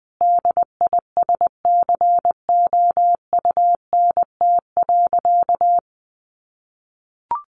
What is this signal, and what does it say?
Morse code, heard as a keyed tone, about 20 wpm, 700 Hz. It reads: DISCOUNT.
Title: Sentences from Top 1200 Words 20wpm